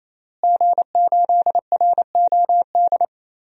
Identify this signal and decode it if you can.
Morse code, heard as a keyed tone, about 27 wpm, 700 Hz. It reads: G8ROD
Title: DXCC-100 Call Signs - 5 Characters - Repeated 3x - 28wpm